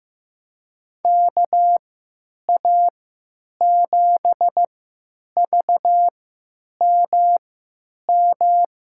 Morse code, heard as a keyed tone, about 16 wpm, spaced as for 9 wpm, 700 Hz. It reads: KA7VMM
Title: USA Call Signs - 6 Characters - 3x Character Spacing - 15wpm